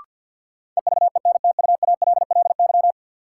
Morse code, heard as a keyed tone, about 53 wpm, 700 Hz. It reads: EVENTUALLY
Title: Sentences from Top 1000 Words - Review Part 6 - New Word Drill - 50wpm